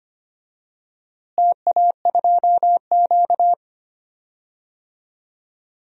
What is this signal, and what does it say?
Morse code, heard as a keyed tone, about 25 wpm, 700 Hz. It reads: TA2Q
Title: DXCC-100 Call Signs - 4 Characters - Repeated 2x - 25wpm